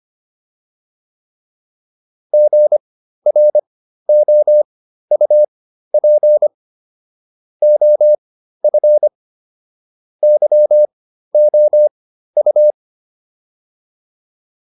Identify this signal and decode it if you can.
Morse code, heard as a keyed tone, about 25 wpm, spaced as for 13 wpm, 600 Hz. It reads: GROUP OF YOU
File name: PR405 13